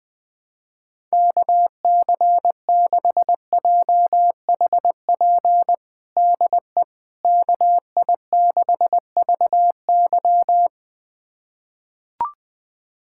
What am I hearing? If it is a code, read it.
Morse code, heard as a keyed tone, about 20 wpm, 700 Hz. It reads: KC6JHP DE KI6VY
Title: USA Callsign DE Callsign 20wpm